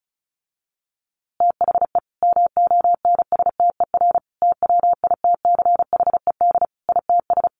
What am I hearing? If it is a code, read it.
Morse code, heard as a keyed tone, about 34 wpm, 700 Hz. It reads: THE MONSTER TWITCHED ITS
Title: A Sound Of Thunder 35wpm